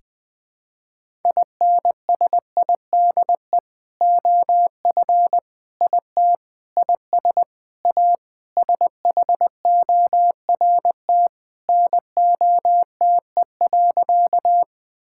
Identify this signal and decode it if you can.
Morse code, heard as a keyed tone, about 21 wpm, 700 Hz. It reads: INSIDE OF IT IS A SHORT NOTE.
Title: Sentences from Top 1000 Words - Review Part 6 - 20wpm